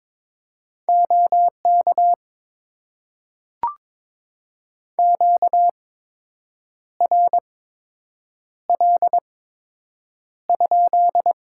Morse code, heard as a keyed tone, about 22 wpm, spaced as for 7 wpm, 700 Hz. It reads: OK QRL?
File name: Top 90 QSO Elements - 8x Character Spacing - 22wpm